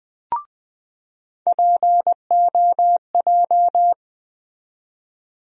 Morse code, heard as a keyed tone, about 20 wpm, 700 Hz. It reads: POJ
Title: USA Call Sign Postfixes - 3 Letters 20wpm